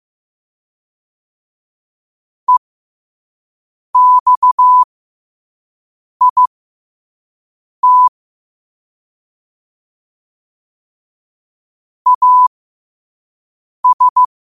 Morse code, heard as a keyed tone, about 15 wpm, spaced as for 5 wpm, 1000 Hz. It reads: EXIT AS
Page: New quotes every day in morse code at 5 Words per minute.